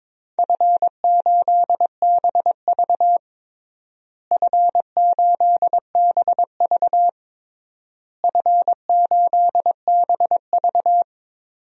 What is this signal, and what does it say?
Morse code, heard as a keyed tone, about 22 wpm, 700 Hz. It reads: F8BV F8BV F8BV
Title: DXCC-100 Call Signs - 4 Characters - Repeated 3x - 22wpm